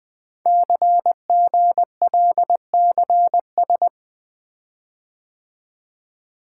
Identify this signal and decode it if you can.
Morse code, heard as a keyed tone, about 20 wpm, 700 Hz. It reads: CGLCS
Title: Code Groups - 5 Letters - 20wpm